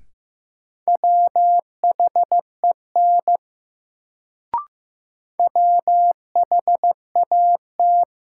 Morse code, heard as a keyed tone, about 15 wpm, 700 Hz. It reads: WHEN WHAT
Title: Sentences from Top 100 Words - New Word Drill - 15wpm